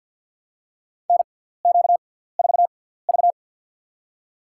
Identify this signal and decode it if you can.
Morse code, heard as a keyed tone, about 53 wpm, spaced as for 17 wpm, 700 Hz. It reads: NQ4V
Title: USA Call Signs - 4 Characters - 6x Character Spacing - 50wpm